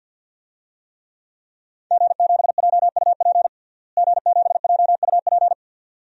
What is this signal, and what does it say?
Morse code, heard as a keyed tone, about 52 wpm, 700 Hz. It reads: G7JUP G7JUP
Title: DXCC-100 Call Signs - 5 Characters - Repeated 2x - 50wpm